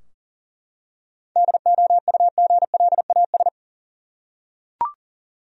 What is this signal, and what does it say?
Morse code, heard as a keyed tone, about 40 wpm, 700 Hz. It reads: DOUGLAS
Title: USA Common Names - 40wpm